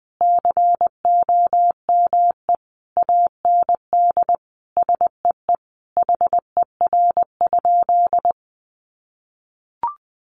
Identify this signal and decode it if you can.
Morse code, heard as a keyed tone, about 20 wpm, 700 Hz. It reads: COME AND SEE HER?
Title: Sentences from Top 100 Words 20wpm